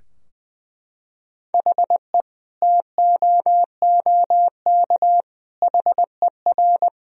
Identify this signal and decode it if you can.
Morse code, heard as a keyed tone, about 21 wpm, 700 Hz. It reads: HE TOOK HER
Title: Sentences from Top 1000 Words - Review Part 2 - 20wpm